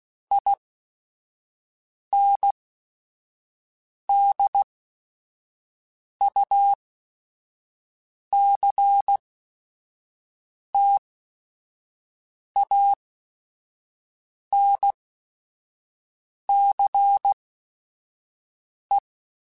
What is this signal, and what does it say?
Morse code, heard as a keyed tone, about 16 wpm, spaced as for 5 wpm, 800 Hz. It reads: INDUCTANCE
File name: Technical Subjects / Technical Subjects-mp3s / Reactance @5